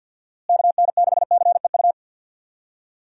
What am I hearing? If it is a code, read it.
Morse code, heard as a keyed tone, about 51 wpm, 700 Hz. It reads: KN6CEU
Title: USA Call Signs - 6 Characters - Repeated 2x - 50wpm